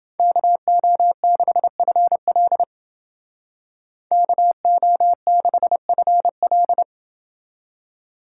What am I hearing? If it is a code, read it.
Morse code, heard as a keyed tone, about 29 wpm, 700 Hz. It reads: KO6FL KO6FL